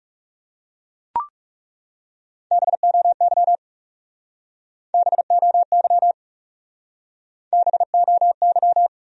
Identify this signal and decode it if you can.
Morse code, heard as a keyed tone, about 40 wpm, 700 Hz. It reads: BOY BOY BOY